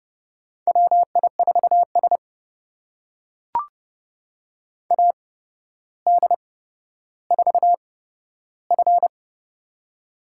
Morse code, heard as a keyed tone, about 29 wpm, spaced as for 13 wpm, 700 Hz. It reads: WI4S AD4F
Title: USA Call Signs - 4 Characters - 8x Character Spacing - 30wpm